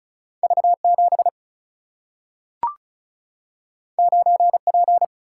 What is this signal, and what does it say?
Morse code, heard as a keyed tone, about 36 wpm, 700 Hz. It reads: V 7 9 P
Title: Single-Letter Single-Number 35wpm